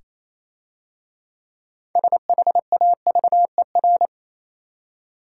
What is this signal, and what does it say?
Morse code, heard as a keyed tone, about 27 wpm, 690 Hz. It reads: SHAVER
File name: Easy Words - Speed-Racing with Hints - 22wpm